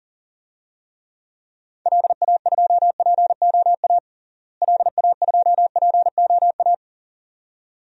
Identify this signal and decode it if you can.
Morse code, heard as a keyed tone, about 37 wpm, 700 Hz. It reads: LA2POA LA2POA
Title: DXCC-100 Call Signs - 6 Characters - Repeated 2x - 40wpm